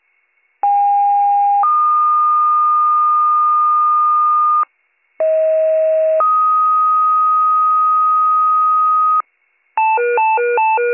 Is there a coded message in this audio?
Yes: two-tone page from 0.6 to 9.2 s, heard as a one-second tone, then a three-second tone, about 810 then 1220 Hz, about 640 then 1250 Hz.